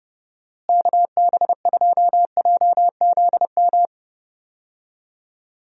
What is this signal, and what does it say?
Morse code, heard as a keyed tone, about 31 wpm, 700 Hz. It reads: KB2JZM
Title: USA Call Signs - 6 Characters - Repeated 3x - 30wpm